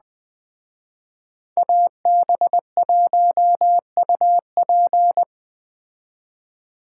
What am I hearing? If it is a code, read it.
Morse code, heard as a keyed tone, about 20 wpm, 700 Hz. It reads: AB1UP
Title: Speed Racing - USA 2x2 Call Signs 17wpm